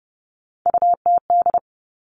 Morse code, heard as a keyed tone, about 31 wpm, 700 Hz. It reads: UTD
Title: Three Challenging Letter-Number Combinations 30wpm